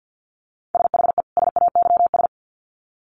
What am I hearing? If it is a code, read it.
Morse code, heard as a keyed tone, about 50 wpm, 700 Hz. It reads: SHE SAYS
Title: Sets of 2 Words 50wpm